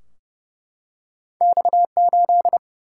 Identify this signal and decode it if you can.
Morse code, heard as a keyed tone, about 30 wpm, 700 Hz. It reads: X 8
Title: Single-Letter Single-Number 30wpm